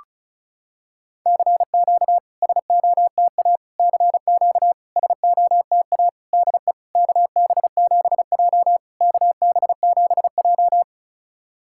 Morse code, heard as a keyed tone, about 34 wpm, 700 Hz. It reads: CQ SOTA CQ SOTA DE KB7J KB7J
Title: SOTA QSO Practice 35wpm